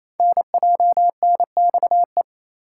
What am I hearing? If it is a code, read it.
Morse code, heard as a keyed tone, about 29 wpm, 700 Hz. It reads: NJNXE
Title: Code Groups - 5 Letters - 28wpm